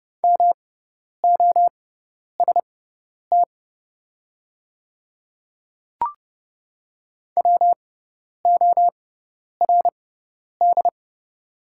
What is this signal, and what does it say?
Morse code, heard as a keyed tone, about 30 wpm, spaced as for 10 wpm, 700 Hz. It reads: MOST WORD
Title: 6x Character Spacing - Top 100 Words - 30wpm